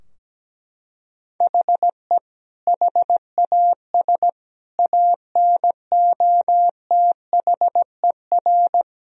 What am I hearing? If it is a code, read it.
Morse code, heard as a keyed tone, about 17 wpm, 700 Hz. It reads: HE HAS ANOTHER
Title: Sentences from Top 1000 Words - Review Part 4 - 17wpm